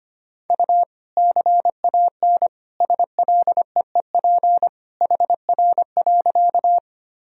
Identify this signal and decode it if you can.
Morse code, heard as a keyed tone, about 25 wpm, 700 Hz. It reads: U CAN SLEEP HR.
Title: Sentences from Top 500 Words and Abbreviations 25wpm